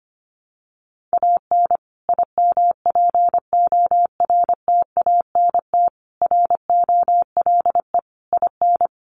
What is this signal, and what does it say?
Morse code, heard as a keyed tone, about 25 wpm, 700 Hz. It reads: AN IMPORTANT ROLE IN
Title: Sets of 5 Words Encore2 25wpm